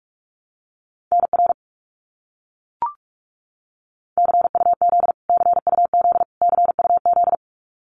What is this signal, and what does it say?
Morse code, heard as a keyed tone, about 44 wpm, 700 Hz. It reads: NR CUZ CUZ CUZ
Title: Abbreviations Repeated 3x - 45wpm